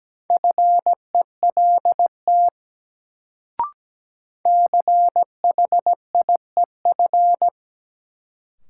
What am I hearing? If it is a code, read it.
Morse code, heard as a keyed tone, about 17 wpm, 700 Hz. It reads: FELT CHIEF
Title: Sets of 5 Words Encore - New Word Drill - 17wpm